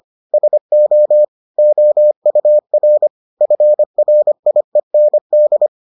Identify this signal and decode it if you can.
Morse code, heard as a keyed tone, about 25 wpm, 600 Hz. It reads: SO OUR FRIEND